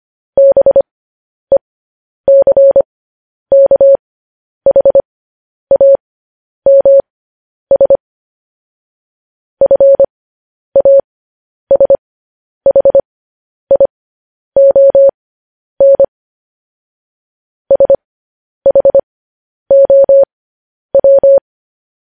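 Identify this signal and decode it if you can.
Morse code, heard as a keyed tone, about 25 wpm, spaced as for 10 wpm, 550 Hz. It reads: BECKHAMS FASHION SHOW
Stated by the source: Morse code news headlines broadcast at 10 wpm.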